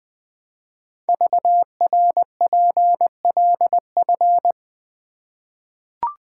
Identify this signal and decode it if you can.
Morse code, heard as a keyed tone, about 20 wpm, 700 Hz. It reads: VRPLF
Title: Code Groups - 5 Letters - 20wpm